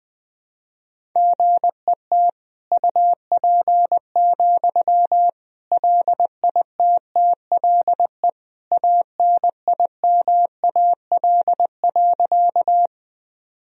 Morse code, heard as a keyed tone, about 20 wpm, 700 Hz. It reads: GET UP, LITTLE ANIMAL.
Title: Sentences from Top 1000 Words - Review Part 1 - 20wpm